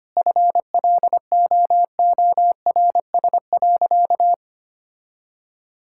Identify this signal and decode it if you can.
Morse code, heard as a keyed tone, about 25 wpm, 700 Hz. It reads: FLOORS.
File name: Easy Sentences with Hints - 25wpm